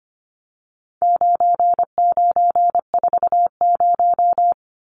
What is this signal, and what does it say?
Morse code, heard as a keyed tone, about 25 wpm, 700 Hz. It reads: 9940
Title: Numbers - 4-Digit 25wpm